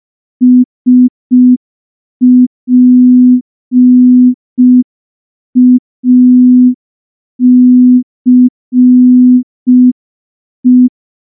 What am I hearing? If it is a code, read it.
Morse code, heard as a keyed tone, about 5 wpm, 250 Hz. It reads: SPACE
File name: morse / morse-training-05WPM